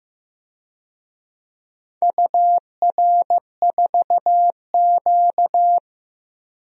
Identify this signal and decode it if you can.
Morse code, heard as a keyed tone, about 15 wpm, 700 Hz. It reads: UR4Q
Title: DXCC-100 Call Signs - 4 Characters - Repeated 2x - 15wpm